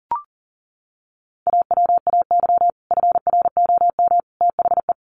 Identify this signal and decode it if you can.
Morse code, heard as a keyed tone, about 40 wpm, 700 Hz. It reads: AWAY FROM THE
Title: Sets of 3 Words 40wpm